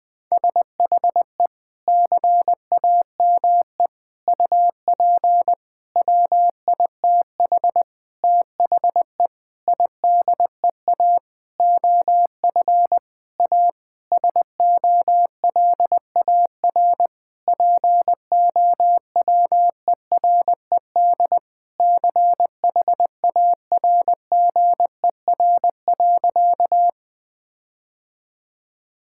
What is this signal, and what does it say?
Morse code, heard as a keyed tone, about 20 wpm, 700 Hz. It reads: SHE CAME UP WITH THE IDEA OF A SOLAR POWERED CHARGER.
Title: Sentences with 6-Word Phrases - 20wpm